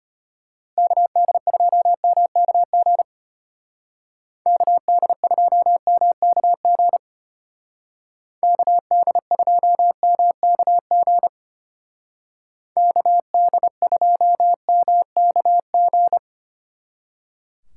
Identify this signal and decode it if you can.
Morse code, heard as a keyed tone, about 33 wpm, 700 Hz. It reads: KD2MKG KD2MKG KD2MKG KD2MKG